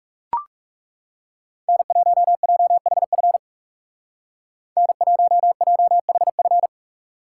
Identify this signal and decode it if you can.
Morse code, heard as a keyed tone, about 37 wpm, 700 Hz. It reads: N1JHF N1JHF
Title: Speed Racing - USA Call Signs 30wpm